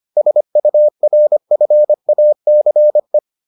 Morse code, heard as a keyed tone, about 25 wpm, 600 Hz. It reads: SURFACE